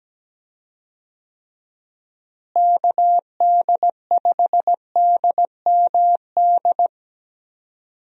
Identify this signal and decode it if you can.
Morse code, heard as a keyed tone, about 17 wpm, 700 Hz. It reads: KD5DMD
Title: USA Call Signs - 6 Characters - Repeated 2x - 17wpm